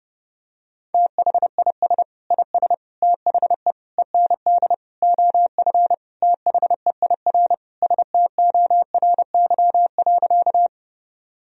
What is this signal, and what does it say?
Morse code, heard as a keyed tone, about 32 wpm, 700 Hz. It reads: THIS IS THE END OF THEIR STORY.
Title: Sentences from Top 1000 Words - Review Part 1 - 30wpm